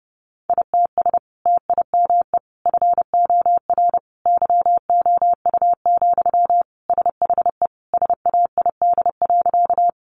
Morse code, heard as a keyed tone, about 30 wpm, 700 Hz. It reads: ITS TIME FOR YOU, SHE SAID.
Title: Sentences from Top 600 Words 30wpm